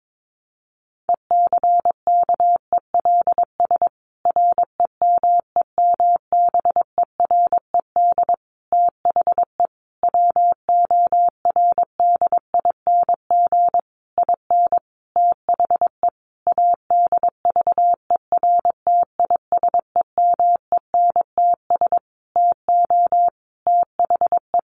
Morse code, heard as a keyed tone, about 22 wpm, 700 Hz. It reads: ECKELS REMEMBERED THE WORDING IN THE ADVERTISEMENTS TO THE
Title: A Sound Of Thunder 22wpm